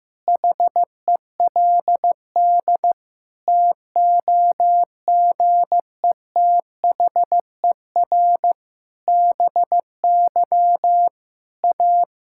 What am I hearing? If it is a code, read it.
Morse code, heard as a keyed tone, about 15 wpm, 700 Hz. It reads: HELD TOGETHER BY A
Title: Sentences from Top 1600 Words 15wpm